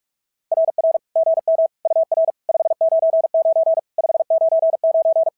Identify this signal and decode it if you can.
Morse code, heard as a keyed tone, about 42 wpm, 660 Hz. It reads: RR GM UR 599 599
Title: SOTA QSO Practice 45wpm